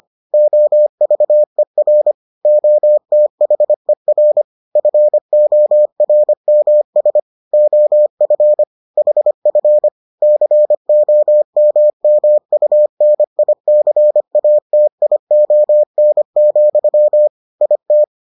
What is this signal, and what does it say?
Morse code, heard as a keyed tone, about 25 wpm, 600 Hz. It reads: OVER OTHER FORMS OF HF COMMUNICATION, IT